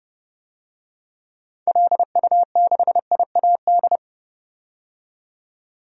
Morse code, heard as a keyed tone, about 31 wpm, 700 Hz. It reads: LU6IAD
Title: DXCC-100 Call Signs - 6 Characters - 30wpm